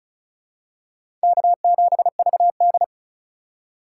Morse code, heard as a keyed tone, about 34 wpm, 700 Hz. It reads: K7VD
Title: USA Call Signs - 4 Characters - 2x Character Spacing - 35wpm